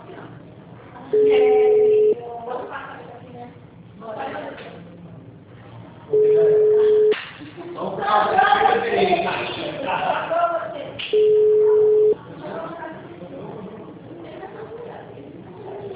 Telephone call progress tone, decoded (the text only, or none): ringback tone